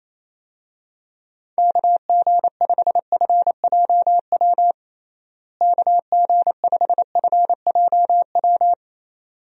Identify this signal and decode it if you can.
Morse code, heard as a keyed tone, about 28 wpm, 700 Hz. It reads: KG5FJW KG5FJW
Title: USA Call Signs - 6 Characters - Repeated 2x - 28wpm